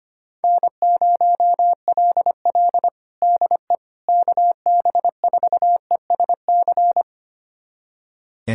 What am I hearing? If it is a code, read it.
Morse code, heard as a keyed tone, about 25 wpm, 700 Hz. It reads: N0LL DE KB4ESC
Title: USA Callsign DE Callsign 25wpm